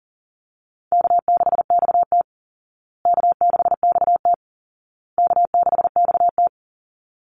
Morse code, heard as a keyed tone, about 40 wpm, 700 Hz. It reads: K6XT K6XT K6XT
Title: Easy Call Signs Repeated 3x - 40wpm